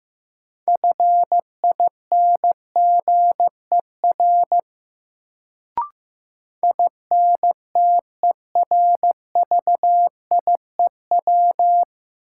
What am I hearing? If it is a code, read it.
Morse code, heard as a keyed tone, about 15 wpm, 700 Hz. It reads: FINGER INTERVIEW
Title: Sentences from Top 1600 Words - New Word Drill - 15wpm